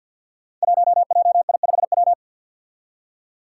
Morse code, heard as a keyed tone, about 47 wpm, 700 Hz. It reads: 1JI5W
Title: Five Challenging Letter-Number Combinations 50wpm